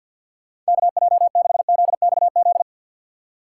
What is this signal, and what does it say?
Morse code, heard as a keyed tone, about 65 wpm, spaced as for 57 wpm, 690 Hz. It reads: KJ6ZXZ
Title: Speed Racing - USA 2x3 Call Signs 50wpm